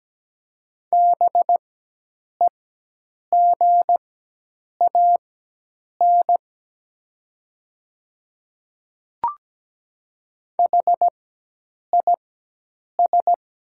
Morse code, heard as a keyed tone, about 17 wpm, spaced as for 8 wpm, 700 Hz. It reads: BEGAN HIS
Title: 4x Character Spacing - Top 300 Words - 17wpm